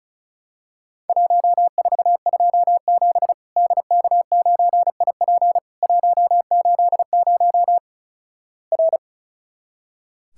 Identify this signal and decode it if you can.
Morse code, heard as a keyed tone, about 36 wpm, 700 Hz. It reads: 1427 DK9IP 180 R
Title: Worked All Europe DX Contest - QTC Exchange - 35wpm